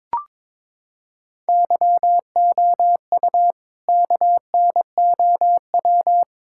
Morse code, heard as a keyed tone, about 21 wpm, 700 Hz. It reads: YOU KNOW
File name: practice / 22wpm / Sentences with 4-Word Phrases - 22wpm